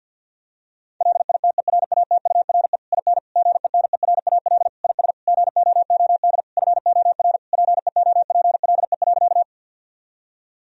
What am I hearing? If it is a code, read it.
Morse code, heard as a keyed tone, about 50 wpm, 700 Hz. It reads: LITERATURE IN GENERAL IS GOOD FOR PEOPLE.
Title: Sentences from Top 1000 Words - Review Part 3 - 50wpm